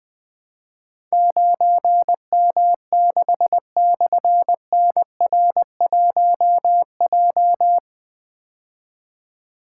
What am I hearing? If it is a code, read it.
Morse code, heard as a keyed tone, about 20 wpm, 700 Hz. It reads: 9M6/NR1J
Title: Difficult Call Signs 20wpm